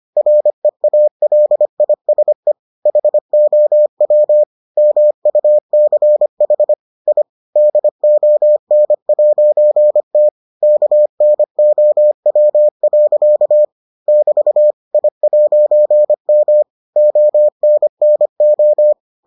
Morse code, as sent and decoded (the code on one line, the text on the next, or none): .-. .- .-.. .. ... / .... --- .-- / -- ..- -.-. .... / .. / -.. --- -. .----. - / -.- -. --- .-- .-.-.- / -...- / .. .----. -- / --- -. -. ---
REALISE HOW MUCH I DON'T KNOW. <BT> I'M ONNO